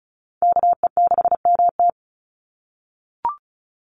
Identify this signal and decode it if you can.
Morse code, heard as a keyed tone, about 34 wpm, 700 Hz. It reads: KE6MT
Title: NA SOTA Chaser Callsigns 35wpm